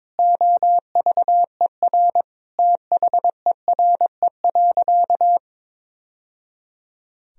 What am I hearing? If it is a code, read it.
Morse code, heard as a keyed tone, about 22 wpm, 700 Hz. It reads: OVER THERE.
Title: Sentences from Top 1000 Words - Review Part 2 - 22wpm